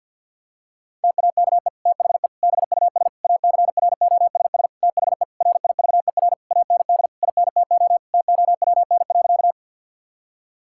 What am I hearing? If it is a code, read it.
Morse code, heard as a keyed tone, about 50 wpm, 700 Hz. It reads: TAKE THE BUS ACROSS THE RIVER AND INTO TOWN.
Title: Sentences from Top 1000 Words - Review Part 6 - 50wpm